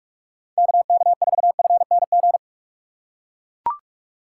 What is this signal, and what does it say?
Morse code, heard as a keyed tone, about 41 wpm, 700 Hz. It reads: KK4FNG
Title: USA Call Signs - 4x Character Spacing - 45wpm